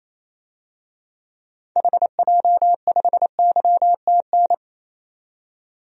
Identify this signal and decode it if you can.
Morse code, heard as a keyed tone, about 28 wpm, 700 Hz. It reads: HJ5YTN
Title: DXCC-100 Call Signs - 6 Characters - 28wpm